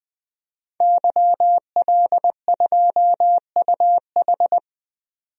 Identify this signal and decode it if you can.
Morse code, heard as a keyed tone, about 20 wpm, 700 Hz. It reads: YL2UH
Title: DXCC-100 Call Signs - 5 Characters - 20wpm